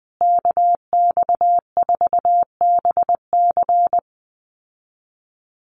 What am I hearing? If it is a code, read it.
Morse code, heard as a keyed tone, about 20 wpm, 700 Hz. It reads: KX4BC
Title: USA Call Signs Repeated 2x 20wpm